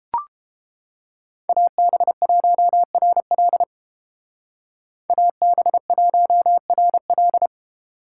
Morse code, heard as a keyed tone, about 30 wpm, 700 Hz. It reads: AB1RL AB1RL